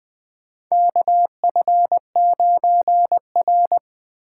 Morse code, heard as a keyed tone, about 21 wpm, 700 Hz. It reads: KF9R
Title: 2023 POTA Hunter Call Signs - 20wpm